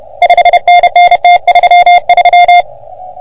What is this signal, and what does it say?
Morse code, heard as a keyed tone, about 31 wpm, 710 Hz. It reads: 5NN T33
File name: OM3PA PILUP CUC 388